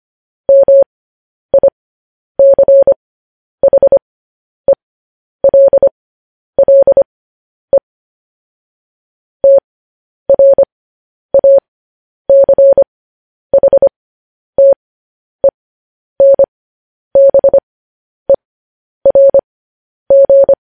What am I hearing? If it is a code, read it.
Morse code, heard as a keyed tone, about 25 wpm, spaced as for 10 wpm, 550 Hz. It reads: MICHELLE TRACHTENBERG
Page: News Headlines in Morse Code at 10 WPM